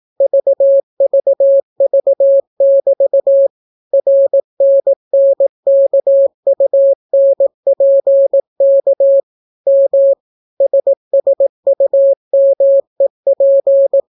Morse code, heard as a keyed tone, about 18 wpm, 550 Hz. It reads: VVV<BT> RNNKUNPK M SSUMEP